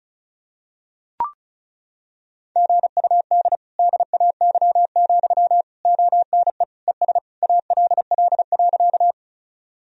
Morse code, heard as a keyed tone, about 36 wpm, 700 Hz. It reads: GUD DAY, ONE ES ALL.
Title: Sentences from Top 200 Words and Abbreviations 35wpm